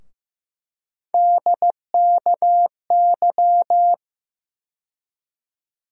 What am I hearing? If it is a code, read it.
Morse code, heard as a keyed tone, about 16 wpm, 700 Hz. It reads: DKY
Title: USA Call Sign Postfixes - 3 Letters 15wpm